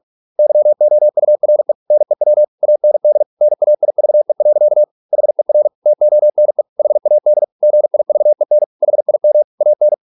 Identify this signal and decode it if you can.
Morse code, heard as a keyed tone, about 46 wpm, 600 Hz. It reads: YOURE NEW AND NAIVE. HER TONE HAD GIVEN HIM AN